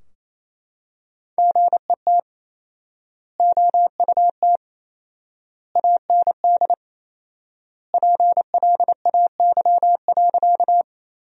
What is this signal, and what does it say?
Morse code, heard as a keyed tone, about 28 wpm, 700 Hz. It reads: GET OUT AND PLAY.
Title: 4x Word-Spacing - Sentences from Top 200 Words - 28wpm